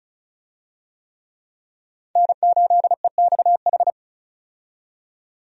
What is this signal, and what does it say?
Morse code, heard as a keyed tone, about 36 wpm, 700 Hz. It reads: N8EXH
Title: USA Call Signs - 5 Characters - 35wpm